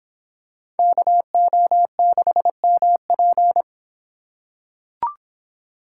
Morse code, heard as a keyed tone, about 26 wpm, 700 Hz. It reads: KO6MP